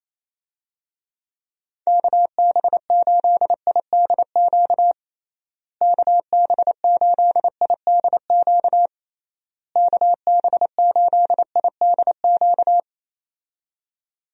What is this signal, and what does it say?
Morse code, heard as a keyed tone, about 28 wpm, 700 Hz. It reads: KB8IDQ KB8IDQ KB8IDQ
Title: USA Call Signs - 6 Characters - Repeated 3x - 28wpm